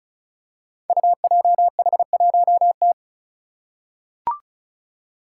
Morse code, heard as a keyed tone, about 36 wpm, 700 Hz. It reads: UJH1T
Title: Five Challenging Letter-Number Combinations 35wpm